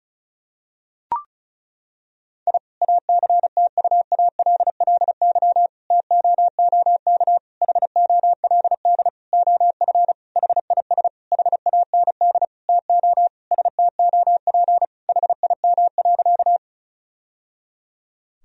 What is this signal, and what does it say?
Morse code, heard as a keyed tone, about 34 wpm, 700 Hz. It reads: I ACTUALLY TOOK HOLD OF HIS HAND TO STOP HIM.
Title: Sentences from Top 1000 Words - Review Part 4 - 35wpm